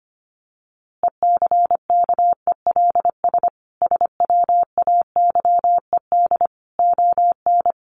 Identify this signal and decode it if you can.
Morse code, heard as a keyed tone, about 25 wpm, 700 Hz. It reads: ECKELS SWAYED ON
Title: A Sound Of Thunder 25wpm